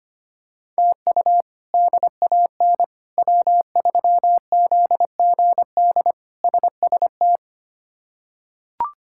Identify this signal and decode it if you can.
Morse code, heard as a keyed tone, about 25 wpm, 700 Hz. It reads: TU DAN W3ZGD SST
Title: K1USN SST QSO Practice - 25wpm